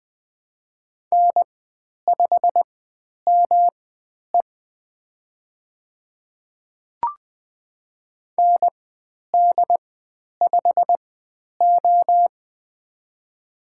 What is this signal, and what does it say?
Morse code, heard as a keyed tone, about 20 wpm, spaced as for 10 wpm, 700 Hz. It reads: N5ME ND5O
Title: 109 Lesson 18g - Letter m - Call Signs 20wpm